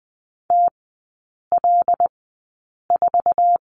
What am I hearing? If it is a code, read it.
Morse code, heard as a keyed tone, about 20 wpm, spaced as for 8 wpm, 700 Hz. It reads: TL4
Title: 074 Lesson 13e - Letter L - Three Characters 20wpm